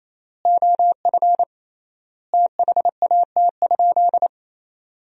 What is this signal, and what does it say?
Morse code, heard as a keyed tone, about 28 wpm, 700 Hz. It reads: OF THAT?
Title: 3x Word-Spacing - Sentences from Top 300 Words - 28wpm